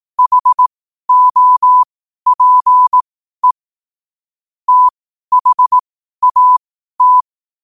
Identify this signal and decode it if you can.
Morse code, heard as a keyed tone, about 18 wpm, spaced as for 12 wpm, 1000 Hz. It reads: HOPE THAT
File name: qotdfiles / 12wpm